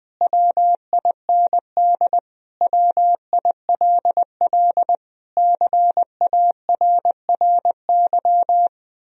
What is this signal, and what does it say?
Morse code, heard as a keyed tone, about 20 wpm, 700 Hz. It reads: WIND WILL CARRY
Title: Sentences from Top 1000 Words - Review Part 3 - 20wpm